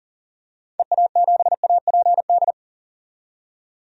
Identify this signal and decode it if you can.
Morse code, heard as a keyed tone, about 40 wpm, 700 Hz. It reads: EA7APD
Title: DXCC-100 Call Signs - 6 Characters - 40wpm